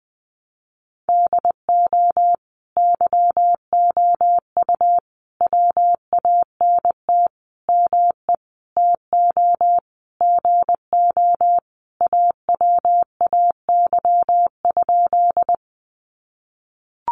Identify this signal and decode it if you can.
Morse code, heard as a keyed tone, about 20 wpm, 700 Hz. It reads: DO YOU WANT ME TO GO AWAY?
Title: Sentences from Top 600 Words 20wpm